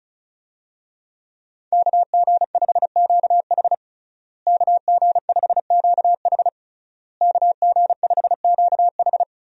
Morse code, heard as a keyed tone, about 36 wpm, 700 Hz. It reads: KG5QH KG5QH KG5QH
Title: USA Call Signs - 5 Characters - Repeated 3x - 35wpm